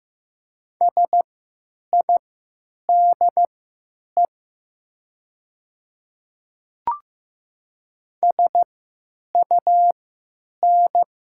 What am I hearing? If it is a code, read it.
Morse code, heard as a keyed tone, about 15 wpm, spaced as for 9 wpm, 700 Hz. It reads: SIDE SUN
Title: 3x Character Spacing - Top 300 Words - 15wpm